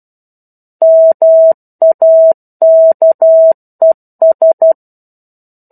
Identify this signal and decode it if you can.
Morse code, heard as a keyed tone, about 12 wpm, 650 Hz. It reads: MAKES
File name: 01-Chapter-01 12 WPM